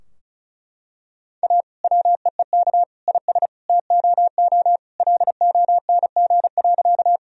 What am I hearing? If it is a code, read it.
Morse code, heard as a keyed tone, about 36 wpm, 700 Hz. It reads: A WEEK IS TOO LONG.
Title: Sentences from Top 1000 Words - Review Part 2 - 35wpm